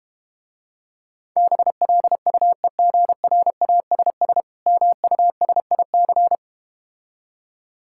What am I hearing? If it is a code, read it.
Morse code, heard as a keyed tone, about 31 wpm, 700 Hz. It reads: BLUEGRASS MUSIC